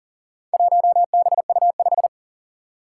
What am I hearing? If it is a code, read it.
Morse code, heard as a keyed tone, about 43 wpm, 700 Hz. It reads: 1BU5
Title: Four Challenging Letter-Number Combinations 40wpm